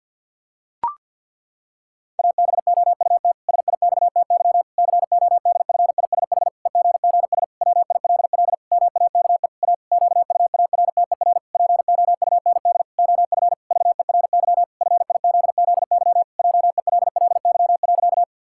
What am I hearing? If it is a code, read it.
Morse code, heard as a keyed tone, about 50 wpm, 700 Hz. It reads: ABOUT SIXTY CODFISH EGGS WILL MAKE A QUARTER POUND OF VERY FIZZY JELLY.
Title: Pangram Challenge v2 - 50wpm